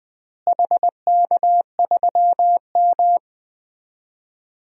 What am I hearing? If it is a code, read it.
Morse code, heard as a keyed tone, about 20 wpm, 700 Hz. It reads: HK3M
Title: DXCC-100 Call Signs - 4 Characters - Repeated 3x - 20wpm